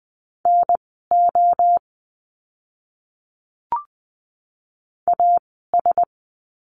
Morse code, heard as a keyed tone, about 20 wpm, spaced as for 14 wpm, 700 Hz. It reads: NO AS
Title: 068 Lesson 12f - Letter d - Words 20wpm